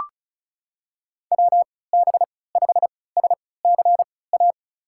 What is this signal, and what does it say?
Morse code, heard as a keyed tone, about 36 wpm, spaced as for 20 wpm, 700 Hz. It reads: WB5SCA
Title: USA Call Signs - 3x Character Spacing - 35wpm